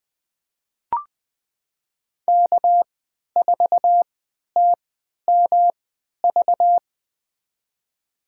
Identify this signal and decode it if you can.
Morse code, heard as a keyed tone, about 20 wpm, spaced as for 11 wpm, 700 Hz. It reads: K4TMV
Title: USA Call Signs - 5 Characters - 3x Character Spacing - 20wpm